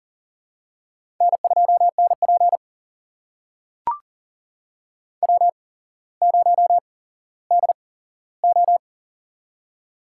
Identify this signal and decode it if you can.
Morse code, heard as a keyed tone, about 40 wpm, spaced as for 17 wpm, 700 Hz. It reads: N2NP W0DO
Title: USA Call Signs - 4 Characters - 8x Character Spacing - 40wpm